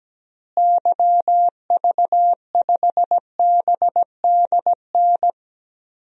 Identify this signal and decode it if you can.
Morse code, heard as a keyed tone, about 17 wpm, 700 Hz. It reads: YV5BDN
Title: DXCC-100 Call Signs - 6 Characters - Repeated 2x - 17wpm